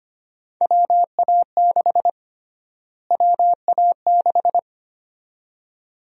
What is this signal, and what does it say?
Morse code, heard as a keyed tone, about 25 wpm, 700 Hz. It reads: WA6 WA6
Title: USA Call Sign Prefixes - Repeated 2x - 25wpm